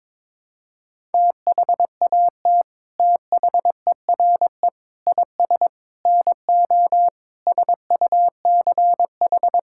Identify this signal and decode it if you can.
Morse code, heard as a keyed tone, about 22 wpm, 700 Hz. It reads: THAT THERE IS NO SUCH
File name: practice / 22wpm / Sets of 8 Words - 22wpm